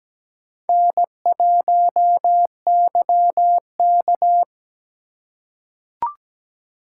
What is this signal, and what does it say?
Morse code, heard as a keyed tone, about 17 wpm, 700 Hz. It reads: N1YK
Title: USA Call Signs - 4 Characters - 2x Character Spacing - 17wpm